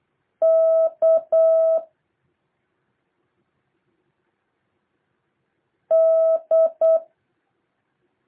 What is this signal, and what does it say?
Morse code, heard as a keyed tone, about 8 wpm, 650 Hz. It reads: K D
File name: jh8cbh / 2023cw